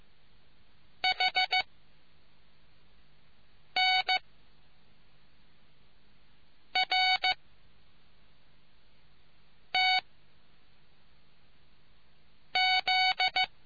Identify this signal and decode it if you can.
Morse code, heard as a keyed tone, about 14 wpm, spaced as for 3 wpm, 2250 Hz. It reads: HNRTZ